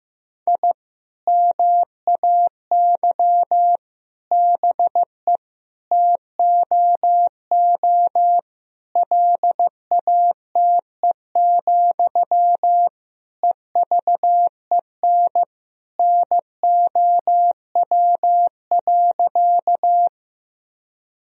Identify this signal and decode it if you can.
Morse code, heard as a keyed tone, about 15 wpm, 700 Hz. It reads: I MAY BE TOO LATE, EVEN NOW.
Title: Sentences from Top 1000 Words - Review Part 1 - 15wpm